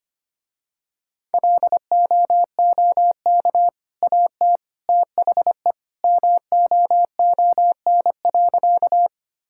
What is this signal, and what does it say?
Morse code, heard as a keyed tone, about 25 wpm, 700 Hz. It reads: LOOK AT THE MOON.
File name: Easy Sentences with Hints - 25wpm